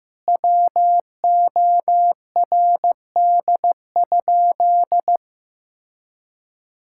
Morse code, heard as a keyed tone, about 15 wpm, 700 Hz. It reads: WORD?
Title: Sentences from Top 1000 Words - Review Part 6 - 15wpm